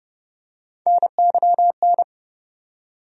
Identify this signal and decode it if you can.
Morse code, heard as a keyed tone, about 30 wpm, 700 Hz. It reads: NYN
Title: USA Call Sign Postfixes - 3 Letters 30wpm